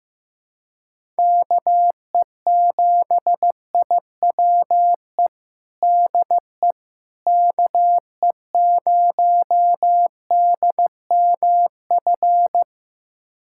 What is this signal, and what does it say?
Morse code, heard as a keyed tone, about 15 wpm, 700 Hz. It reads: KE7IWE DE KE0DMF
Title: USA Callsign DE Callsign 15wpm